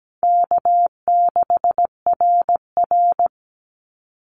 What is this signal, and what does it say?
Morse code, heard as a keyed tone, about 17 wpm, 700 Hz. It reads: K6RR
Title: Call Signs Repeated 3x - 17wpm